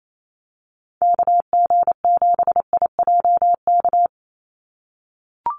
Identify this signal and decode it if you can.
Morse code, heard as a keyed tone, about 28 wpm, 700 Hz. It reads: KG7IJK
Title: USA Call Signs 28wpm